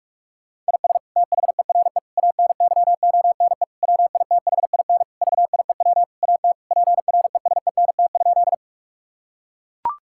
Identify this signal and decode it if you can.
Morse code, heard as a keyed tone, about 42 wpm, 700 Hz. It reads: IS THERE ANYONE WITHIN VIEW AT PRESENT?
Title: Sentences from Top 1000 Words - Review Part 6 - 45wpm